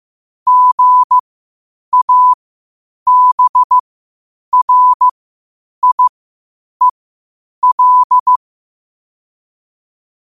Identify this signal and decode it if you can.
Morse code, heard as a keyed tone, about 15 wpm, spaced as for 8 wpm, 1000 Hz. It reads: GABRIEL
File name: qotdfiles / 8wpm